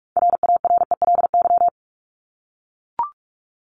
Morse code, heard as a keyed tone, about 44 wpm, 700 Hz. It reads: RARELY
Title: LY Suffix - Top 100 Words with Base Words 45wpm